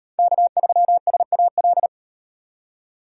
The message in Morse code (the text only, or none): K3SAL